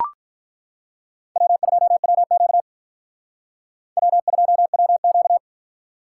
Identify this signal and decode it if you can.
Morse code, heard as a keyed tone, about 43 wpm, 700 Hz. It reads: W2WQ W2WQ
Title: Speed Racing - USA 1x2 Call Signs 35wpm